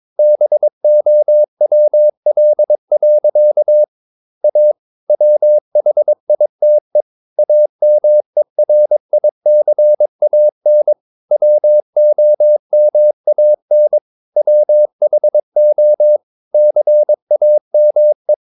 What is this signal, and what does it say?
Morse code, heard as a keyed tone, about 22 wpm, 600 Hz. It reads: BOWL. A WHITE AMERICAN WOMAN WHO CAME
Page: Romanen Never af Ken Follet på engelsk.